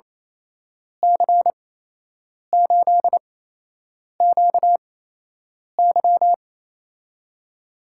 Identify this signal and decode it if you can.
Morse code, heard as a keyed tone, about 28 wpm, spaced as for 8 wpm, 700 Hz. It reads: C8QY
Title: USA Call Signs - 5 Characters - 8x Character Spacing - 28wpm